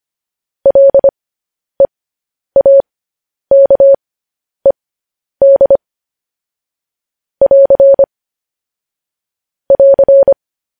Morse code, heard as a keyed tone, about 25 wpm, spaced as for 10 wpm, 550 Hz. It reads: LEAKED <AR> <AR>